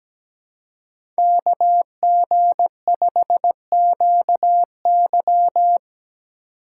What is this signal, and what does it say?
Morse code, heard as a keyed tone, about 17 wpm, 700 Hz. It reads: KG5QY
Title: USA Call Signs - 5 Characters - 2x Character Spacing - 17wpm